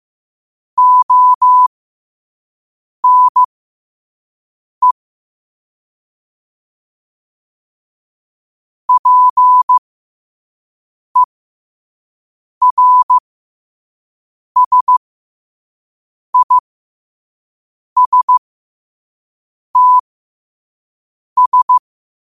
Morse code, heard as a keyed tone, about 15 wpm, spaced as for 5 wpm, 1000 Hz. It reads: ONE PERSISTS
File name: qotdfiles / 5wpm